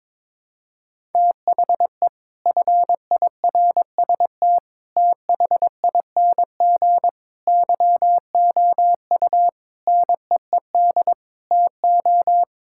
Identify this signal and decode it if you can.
Morse code, heard as a keyed tone, about 22 wpm, 700 Hz. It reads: THE FIRST THING YOU NEED TO
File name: practice / 22wpm / Sets of 8 Words - 22wpm